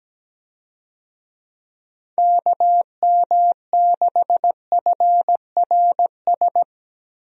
Morse code, heard as a keyed tone, about 17 wpm, 700 Hz. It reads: KM6FRS
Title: USA Call Signs - 6 Characters - Repeated 3x - 17wpm